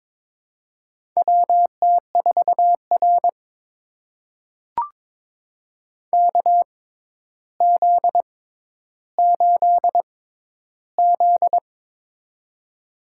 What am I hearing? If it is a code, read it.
Morse code, heard as a keyed tone, about 22 wpm, spaced as for 11 wpm, 700 Hz. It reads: WT4R KZ8Z
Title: USA Call Signs - 4 Characters - 6x Character Spacing - 22wpm